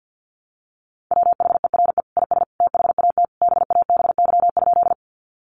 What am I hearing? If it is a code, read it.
Morse code, heard as a keyed tone, about 50 wpm, 700 Hz. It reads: WHERE IS THAT BABY?
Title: Sentences from Top 1300 Words 50wpm